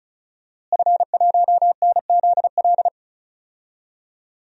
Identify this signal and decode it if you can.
Morse code, heard as a keyed tone, about 36 wpm, 700 Hz. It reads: F1NZL
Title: DXCC-100 Call Signs - 5 Characters - Repeated 3x - 35wpm